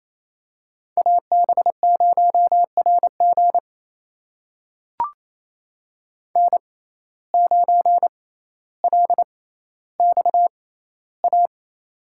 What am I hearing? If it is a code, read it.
Morse code, heard as a keyed tone, about 28 wpm, spaced as for 14 wpm, 700 Hz. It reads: AB0RG N9LXA
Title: USA Call Signs - 5 Characters - 6x Character Spacing - 28wpm